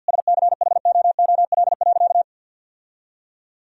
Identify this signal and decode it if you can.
Morse code, heard as a keyed tone, about 50 wpm, 700 Hz. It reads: SCHOOL.
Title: Sentences from Top 1000 Words - Review Part 1 - 50wpm